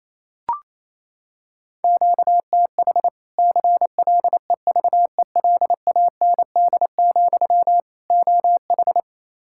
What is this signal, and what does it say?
Morse code, heard as a keyed tone, about 28 wpm, 700 Hz. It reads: QTH CLEVELAND, OH
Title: Practice QSO - Exchange 1 - Beginning - 28wpm